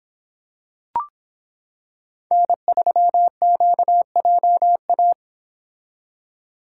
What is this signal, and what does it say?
Morse code, heard as a keyed tone, about 25 wpm, 700 Hz. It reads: N3QJA